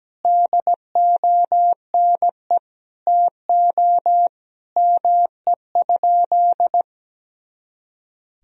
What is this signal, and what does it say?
Morse code, heard as a keyed tone, about 17 wpm, 700 Hz. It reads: DONE TO ME?
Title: Sentences from Top 500 Words and Abbreviations 17wpm